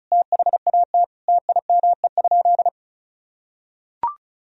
Morse code, heard as a keyed tone, about 34 wpm, 690 Hz. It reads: THAT TIME?
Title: Sentences from Top 200 Words and Abbreviations 35wpm